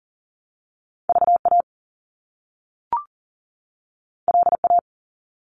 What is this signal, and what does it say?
Morse code, heard as a keyed tone, about 42 wpm, 700 Hz. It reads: VA LA
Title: US States 40wpm